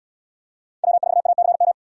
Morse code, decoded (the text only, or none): W4ACM